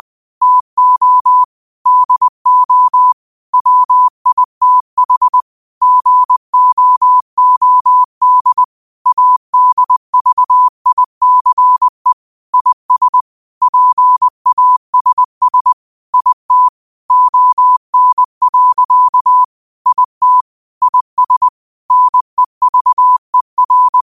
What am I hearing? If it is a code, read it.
Morse code, heard as a keyed tone, about 20 wpm, 1000 Hz. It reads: TO DO WITH GOOD ADVICE IS PASS IT ON. IT IS NEVER